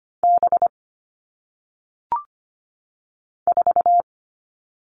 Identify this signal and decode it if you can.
Morse code, heard as a keyed tone, about 25 wpm, 700 Hz. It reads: B 4
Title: All Cut Numbers - 1-Digit 25wpm